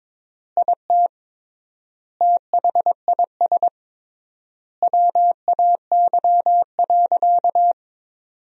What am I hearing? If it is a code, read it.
Morse code, heard as a keyed tone, about 22 wpm, 700 Hz. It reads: IT THIS WAY.
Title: 3x Word-Spacing - Sentences from Top 100 Words - 22wpm